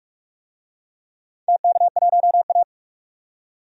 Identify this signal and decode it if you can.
Morse code, heard as a keyed tone, about 41 wpm, 700 Hz. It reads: TK1A
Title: DXCC-100 Call Signs - 4 Characters - 45wpm